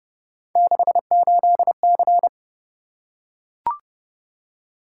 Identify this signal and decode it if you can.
Morse code, heard as a keyed tone, about 28 wpm, 700 Hz. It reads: 68C